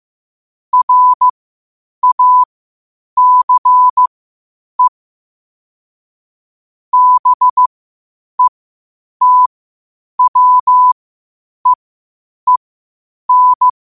Morse code, heard as a keyed tone, about 15 wpm, spaced as for 8 wpm, 1000 Hz. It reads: RACE BETWEEN
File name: qotdfiles / 8wpm